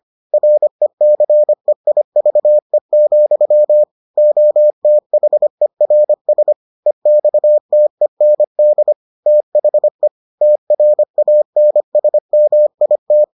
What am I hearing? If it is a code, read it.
Morse code, heard as a keyed tone, about 25 wpm, 600 Hz. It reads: RECEIVE, OTHERS EXTEND THE TRANSMIT